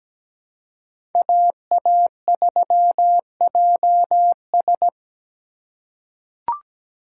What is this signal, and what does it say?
Morse code, heard as a keyed tone, about 17 wpm, 700 Hz. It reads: AA3JS
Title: USA Call Signs - 5 Characters - 4x Character Spacing - 17wpm